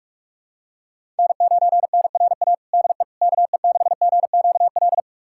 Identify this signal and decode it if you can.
Morse code, heard as a keyed tone, about 46 wpm, 700 Hz. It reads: N9NRA DE KE6GQL
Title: USA Callsign DE Callsign 45wpm